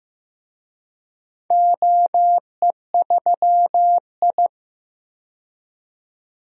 Morse code, heard as a keyed tone, about 15 wpm, 700 Hz. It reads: OE3I
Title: DXCC-100 Call Signs - 4 Characters - 15wpm